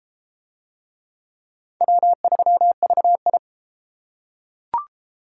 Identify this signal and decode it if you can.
Morse code, heard as a keyed tone, about 31 wpm, 700 Hz. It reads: W3VI